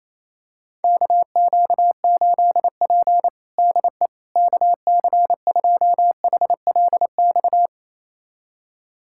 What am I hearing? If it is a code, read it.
Morse code, heard as a keyed tone, about 28 wpm, 700 Hz. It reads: KQ8P DE KC2HLX
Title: USA Callsign DE Callsign 28wpm